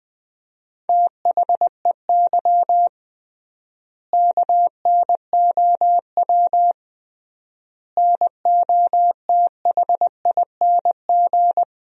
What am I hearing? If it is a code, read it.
Morse code, heard as a keyed tone, about 20 wpm, 700 Hz. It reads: THEY KNOW NOTHING
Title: 3x Word-Spacing - Sentences from Top 500 Words - 20wpm